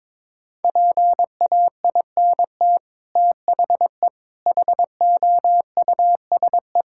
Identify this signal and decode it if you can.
Morse code, heard as a keyed tone, about 22 wpm, 700 Hz. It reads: PAINT THE HOUSE
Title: Sentences from Top 1000 Words - Review Part 6 - 22wpm